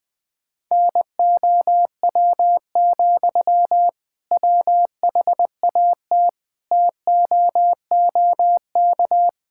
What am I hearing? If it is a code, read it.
Morse code, heard as a keyed tone, about 20 wpm, 700 Hz. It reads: NOW, WHAT TOOK
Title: Sentences from Top 1600 Words 20wpm